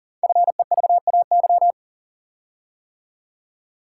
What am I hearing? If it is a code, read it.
Morse code, heard as a keyed tone, about 43 wpm, 700 Hz. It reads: FEVAY
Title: Code Groups - 5 Letter:Numbers - 40wpm